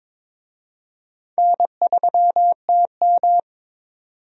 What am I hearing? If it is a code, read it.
Morse code, heard as a keyed tone, about 22 wpm, 700 Hz. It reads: N3TM
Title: USA Call Signs - 4 Characters - 22wpm